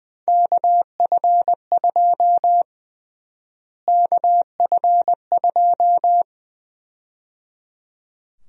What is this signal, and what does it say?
Morse code, heard as a keyed tone, about 20 wpm, 700 Hz. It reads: KF2 KF2
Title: USA Call Sign Prefixes - Repeated 2x - 20wpm